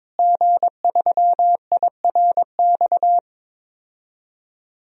Morse code, heard as a keyed tone, about 22 wpm, 700 Hz. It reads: G3IRX
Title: DXCC-100 Call Signs - 5 Characters - Repeated 3x - 22wpm